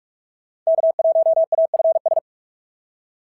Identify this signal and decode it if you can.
Morse code, heard as a keyed tone, about 44 wpm, 690 Hz. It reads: K1AFS
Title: SOTA QSO Practice 45wpm